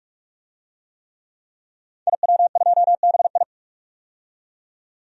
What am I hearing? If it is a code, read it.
Morse code, heard as a keyed tone, about 46 wpm, 700 Hz. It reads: IW2BI
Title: DXCC-100 Call Signs - 5 Characters - 45wpm